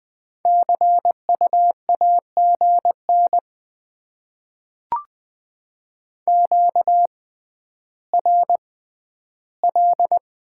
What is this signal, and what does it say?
Morse code, heard as a keyed tone, about 20 wpm, spaced as for 11 wpm, 700 Hz. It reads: CU AGN QRL
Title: Top 20 QSO Elements - 6x Character Spacing - 20wpm